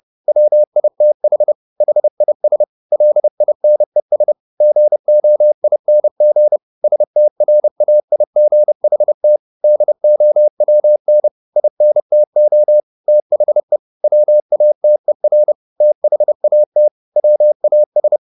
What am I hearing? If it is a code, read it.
Morse code, heard as a keyed tone, about 30 wpm, 600 Hz. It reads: WITH HIS LINES GOING STRAIGHT DOWN INTO THE WATER THAT WAS
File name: the old man and the sea / the old man and the sea 1 30wpm